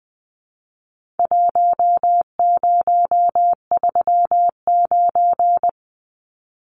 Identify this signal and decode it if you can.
Morse code, heard as a keyed tone, about 20 wpm, 700 Hz. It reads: 1039
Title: Numbers - Contest Style - 4-Digit 20wpm